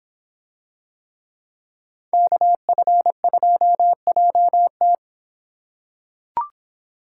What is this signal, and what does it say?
Morse code, heard as a keyed tone, about 24 wpm, 700 Hz. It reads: KF2JT